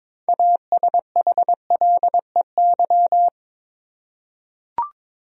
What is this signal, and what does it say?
Morse code, heard as a keyed tone, about 22 wpm, 700 Hz. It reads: ASHLEY
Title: USA Common Names - 22wpm